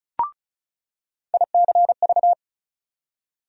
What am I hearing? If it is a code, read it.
Morse code, heard as a keyed tone, about 36 wpm, 700 Hz. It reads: ICV
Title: USA Call Sign Postfixes - 3 Letters 35wpm